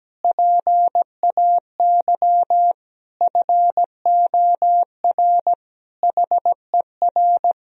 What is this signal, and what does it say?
Morse code, heard as a keyed tone, about 17 wpm, 700 Hz. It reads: PAY FOR HER
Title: Sentences from Top 1000 Words - Review Part 4 - 17wpm